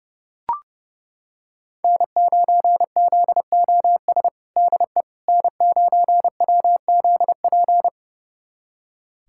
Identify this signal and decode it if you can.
Morse code, heard as a keyed tone, about 32 wpm, 700 Hz. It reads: N9ZOS DE N9WZP
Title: USA Callsign DE Callsign - 5 Characters - 30wpm